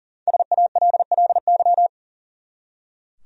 Morse code, heard as a keyed tone, about 40 wpm, 700 Hz. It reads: SALLY
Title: USA Common Names - 40wpm